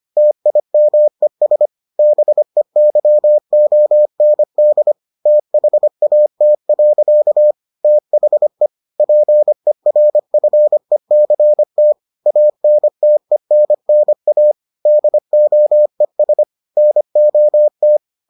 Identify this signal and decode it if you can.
Morse code, heard as a keyed tone, about 25 wpm, 600 Hz. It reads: TIMES BEYOND THAT. THE PERFECT ANTENNA DOES NOT